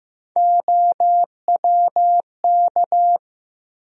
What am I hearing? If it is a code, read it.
Morse code, heard as a keyed tone, about 15 wpm, 700 Hz. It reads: OWK
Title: USA Call Sign Postfixes - 3 Letters 15wpm